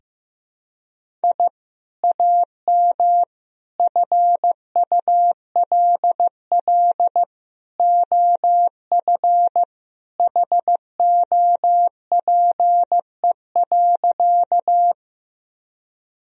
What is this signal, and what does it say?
Morse code, heard as a keyed tone, about 15 wpm, 700 Hz. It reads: I AM FULL OF HOPE.
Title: Sentences from Top 1000 Words - Review Part 3 - 15wpm